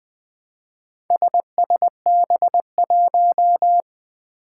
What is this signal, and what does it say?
Morse code, heard as a keyed tone, about 20 wpm, 700 Hz. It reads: SSB1
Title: Four Challenging Letter-Number Combinations 20wpm